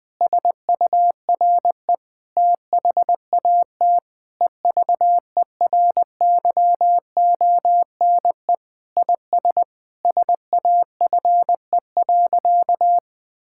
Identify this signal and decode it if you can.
Morse code, heard as a keyed tone, about 20 wpm, 700 Hz. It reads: SURE THAT EVERYONE IS SAFE.
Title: Sentences with 5-Word Phrases - Spoken Beginning - 20wpm